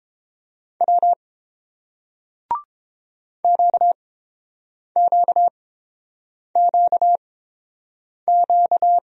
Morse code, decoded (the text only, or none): W QQQQ